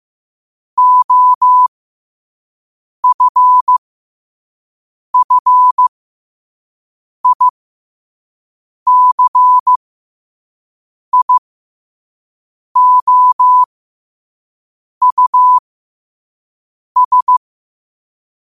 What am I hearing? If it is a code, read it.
Morse code, heard as a keyed tone, about 15 wpm, spaced as for 5 wpm, 1000 Hz. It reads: OFFICIOUS
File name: qotdfiles / 5wpm